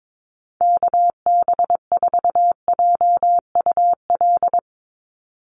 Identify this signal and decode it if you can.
Morse code, heard as a keyed tone, about 22 wpm, 700 Hz. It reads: KB4JUL
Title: USA Call Signs 22wpm